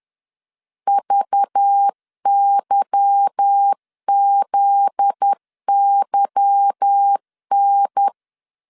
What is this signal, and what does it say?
Morse code, heard as a keyed tone, about 10 wpm, 800 Hz. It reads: VYZYN
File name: LEZIONI / 8 YJ45 / 8 Y 10wpm 2615